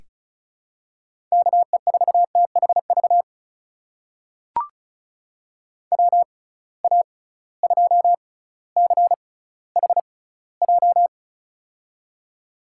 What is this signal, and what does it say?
Morse code, heard as a keyed tone, about 36 wpm, spaced as for 18 wpm, 700 Hz. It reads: KE4THV WA2CHJ
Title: USA Call Signs - 6 Characters - 6x Character Spacing - 35wpm